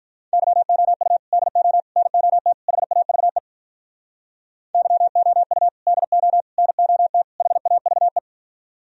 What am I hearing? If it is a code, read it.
Morse code, heard as a keyed tone, about 45 wpm, 700 Hz. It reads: YOU DO NOT HAVE YOU DO NOT HAVE